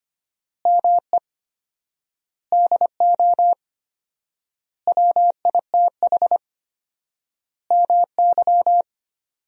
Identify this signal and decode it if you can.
Morse code, heard as a keyed tone, about 25 wpm, 700 Hz. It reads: ME DO WITH MY
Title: 4x Word-Spacing - Sentences from Top 300 Words - 25wpm